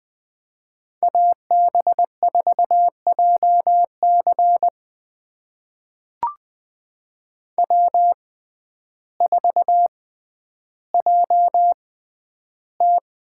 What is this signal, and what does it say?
Morse code, heard as a keyed tone, about 20 wpm, spaced as for 11 wpm, 700 Hz. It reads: AB4JC W4JT
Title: USA Call Signs - 5 Characters - 6x Character Spacing - 20wpm